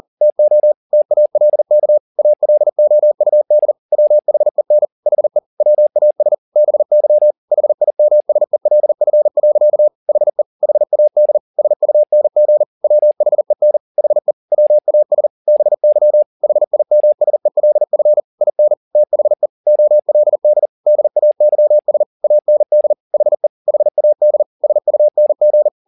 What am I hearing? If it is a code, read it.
Morse code, heard as a keyed tone, about 40 wpm, 600 Hz. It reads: TO TALK ALOUD WHEN HE WAS BY HIMSELF. HE HAD SUNG WHEN HE WAS BY HIMSELF IN THE OLD DAYS AND HE HAD SUNG